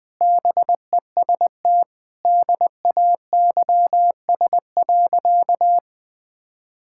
Morse code, heard as a keyed tone, about 20 wpm, 700 Hz. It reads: BEST DAYS.
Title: Sentences with 3-Word Phrases - Spoken Beginning - 20wpm